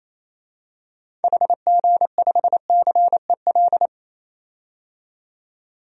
Morse code, heard as a keyed tone, about 28 wpm, 700 Hz. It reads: HG5CEL
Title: DXCC-100 Call Signs - 6 Characters - 28wpm